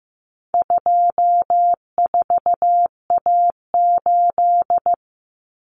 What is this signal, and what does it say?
Morse code, heard as a keyed tone, about 15 wpm, 700 Hz. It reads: 24A8
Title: Common Cut Numbers - 4-Digit 15wpm